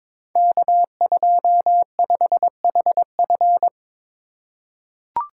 Morse code, heard as a keyed tone, about 22 wpm, 700 Hz. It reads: K25HF
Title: Code Groups - 5 Letter:Numbers - 22wpm